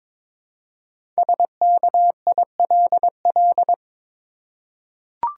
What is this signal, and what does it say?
Morse code, heard as a keyed tone, about 22 wpm, 700 Hz. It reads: SKILL
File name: practice / 22wpm / 5 Letter Words - First 3 Letters Given - 22wpm